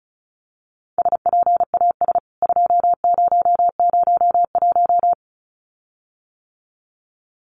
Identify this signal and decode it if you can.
Morse code, heard as a keyed tone, about 34 wpm, 700 Hz. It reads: SPAS 2001
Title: CWT Exchange 35wpm